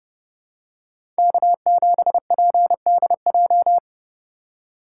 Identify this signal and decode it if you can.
Morse code, heard as a keyed tone, about 30 wpm, 700 Hz. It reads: K7PDJ
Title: USA Call Signs - 5 Characters - 3x Character Spacing - 30wpm